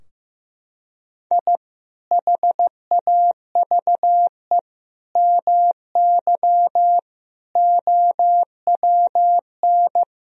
Morse code, heard as a keyed tone, about 16 wpm, 700 Hz. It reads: I HAVE MY OWN
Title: Sentences from Top 1000 Words - Review Part 2 - 15wpm